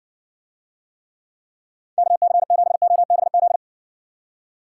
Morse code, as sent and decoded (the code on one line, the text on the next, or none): -.- -.-. --... --- -... --..
KC7OBZ